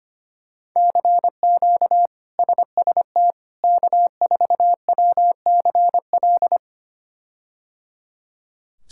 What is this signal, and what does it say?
Morse code, heard as a keyed tone, about 25 wpm, 700 Hz. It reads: CQ SST K4WCL
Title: K1USN SST QSO Practice - 25wpm